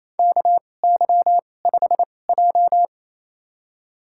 Morse code, heard as a keyed tone, about 28 wpm, spaced as for 20 wpm, 700 Hz. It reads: KY5J
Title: USA Call Signs - 4 Characters - 2x Character Spacing - 28wpm